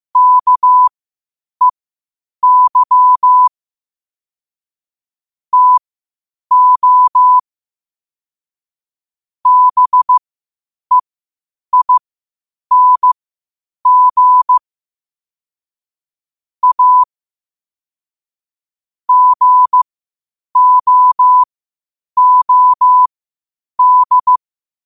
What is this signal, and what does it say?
Morse code, heard as a keyed tone, about 15 wpm, spaced as for 8 wpm, 1000 Hz. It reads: KEY TO BEING A GOOD
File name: qotdfiles / 8wpm